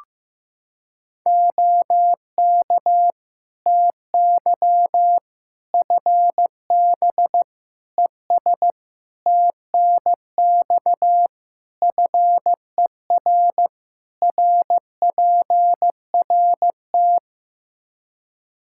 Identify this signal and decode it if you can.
Morse code, heard as a keyed tone, about 15 wpm, 700 Hz. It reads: OK TY FB ES TNX FER RPRT
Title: Practice QSO - Exchange 2 - Info - 15wpm